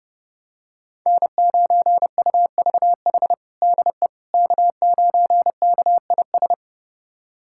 Morse code, heard as a keyed tone, about 30 wpm, 700 Hz. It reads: N9UVH DE K9KIS
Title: USA Callsign DE Callsign - 5 Characters - 30wpm